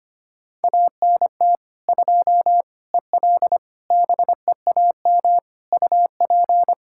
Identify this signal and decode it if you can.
Morse code, heard as a keyed tone, about 25 wpm, 700 Hz. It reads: ANT 2 EL BEAM UP
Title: Practice QSO - Exchange 2 - Info - 25wpm